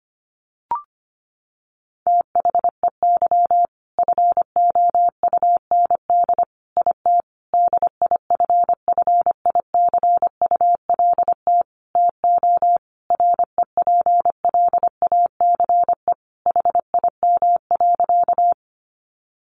Morse code, heard as a keyed tone, about 25 wpm, 700 Hz. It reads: THEY FOUND IT DIFFICULT TO REPLACE HIM.
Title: Sentences from Top 1400 Words 25wpm